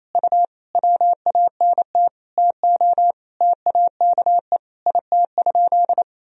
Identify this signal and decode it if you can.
Morse code, heard as a keyed tone, about 28 wpm, 700 Hz. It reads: U WANT TO TAKE IT?
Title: Sentences from Top 300 Words and Abbreviations 28wpm